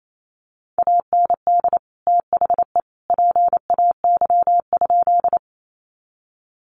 Morse code, heard as a keyed tone, about 28 wpm, 700 Hz. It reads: AND THE PAY?
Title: Sentences from Top 700 Words 28wpm